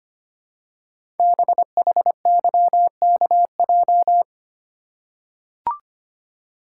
Morse code, heard as a keyed tone, about 25 wpm, 700 Hz. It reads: BHYKJ
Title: Code Groups - 5 Letters - 25wpm